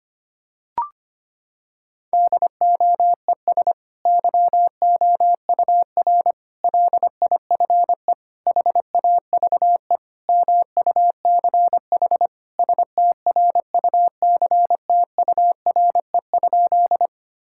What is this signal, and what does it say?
Morse code, heard as a keyed tone, about 25 wpm, 700 Hz. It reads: DOES YOUR LIFE HAVE MUCH STRUCTURE?
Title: Sentences from Top 1000 Words - Review Part 6 - 25wpm